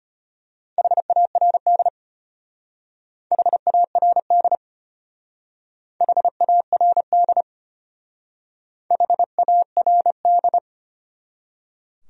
Morse code, heard as a keyed tone, about 31 wpm, 700 Hz. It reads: HARD HARD HARD HARD
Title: Speed-Racing Top 300 Words 25wpm